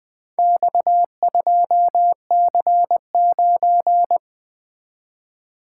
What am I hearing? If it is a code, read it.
Morse code, heard as a keyed tone, about 22 wpm, 700 Hz. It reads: X2C9
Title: Speed-Racing - Four Letter-Number Combinations - 20wpm